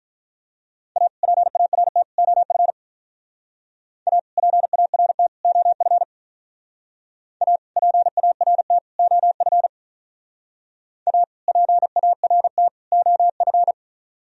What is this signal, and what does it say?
Morse code, heard as a keyed tone, about 44 wpm, 700 Hz. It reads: A PART OF A PART OF A PART OF A PART OF
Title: Speed-Racing - Sets of 3 Words 35wpm